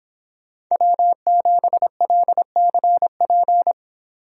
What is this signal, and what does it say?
Morse code, heard as a keyed tone, about 26 wpm, 700 Hz. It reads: W7LCP